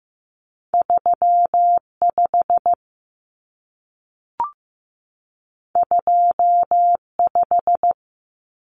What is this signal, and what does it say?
Morse code, heard as a keyed tone, about 15 wpm, 700 Hz. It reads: 35 25
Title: Numbers - 2-Digit 15wpm